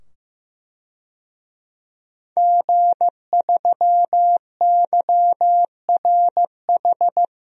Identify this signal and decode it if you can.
Morse code, heard as a keyed tone, about 15 wpm, 700 Hz. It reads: G3YRH
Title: DXCC-100 Call Signs - 5 Characters - 15wpm